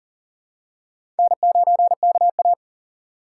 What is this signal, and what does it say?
Morse code, heard as a keyed tone, about 40 wpm, 700 Hz. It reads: N9KA
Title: USA Call Signs - 3x Character Spacing - 40wpm